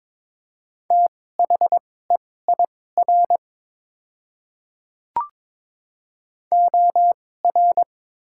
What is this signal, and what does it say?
Morse code, heard as a keyed tone, about 22 wpm, spaced as for 16 wpm, 700 Hz. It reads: THEIR OR
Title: 2x Character Spacing - Top 200 Words - 22wpm